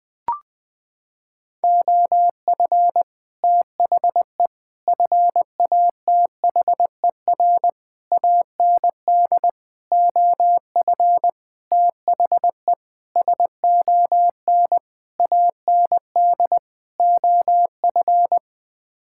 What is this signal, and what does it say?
Morse code, heard as a keyed tone, about 20 wpm, 700 Hz. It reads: OF THE FATHER AND OF THE SON AND OF
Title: Sets of 9 Words - 20wpm